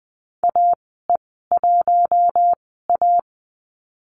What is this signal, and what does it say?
Morse code, heard as a keyed tone, about 20 wpm, spaced as for 14 wpm, 700 Hz. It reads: AE1A
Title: 042 Lesson 8g - Number 1 - Call Signs 20wpm